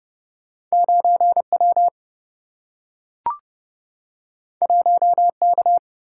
Morse code, heard as a keyed tone, about 30 wpm, 700 Hz. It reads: 9W 1K
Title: Single-Letter Single-Number 30wpm